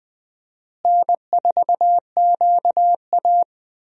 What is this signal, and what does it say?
Morse code, heard as a keyed tone, about 20 wpm, 700 Hz. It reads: N4QA
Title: USA Call Signs - 4 Characters - Repeated 3x - 20wpm